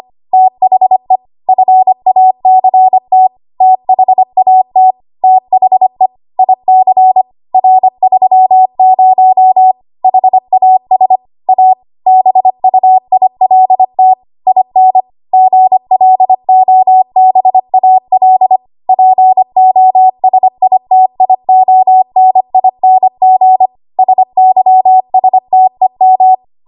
Morse code, heard as a keyed tone, about 25 wpm, 750 Hz. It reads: THE FACT THAT THE IC R30 HAS A BUILT IN GLOBAL POSITIONING SYSTEM
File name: Archive / 25 WPM / 25WPM